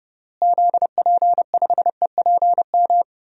Morse code, heard as a keyed tone, about 30 wpm, 700 Hz. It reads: ZP5EPM
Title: DXCC-100 Call Signs - 6 Characters - Repeated 2x - 30wpm